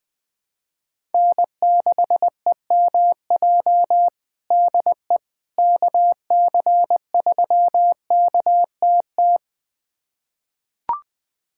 Morse code, heard as a keyed tone, about 20 wpm, 700 Hz. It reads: N6EMJ DE KC3KTT
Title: USA Callsign DE Callsign 20wpm